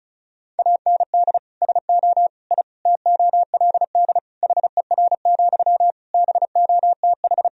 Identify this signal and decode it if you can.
Morse code, heard as a keyed tone, about 34 wpm, 700 Hz. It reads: AND SO I TOLD HER, BOTH
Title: Sentences from Top 1000 Words - Review Part 4 - 35wpm